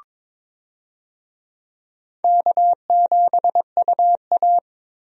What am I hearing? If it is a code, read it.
Morse code, heard as a keyed tone, about 22 wpm, 700 Hz. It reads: K7UA
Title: USA Call Signs - 4 Characters - Repeated 2x - 22wpm